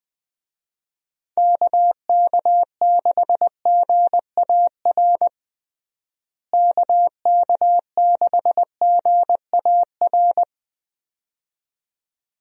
Morse code, heard as a keyed tone, about 21 wpm, 700 Hz. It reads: KK6GAR KK6GAR
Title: USA Call Signs - 6 Characters - Repeated 2x - 20wpm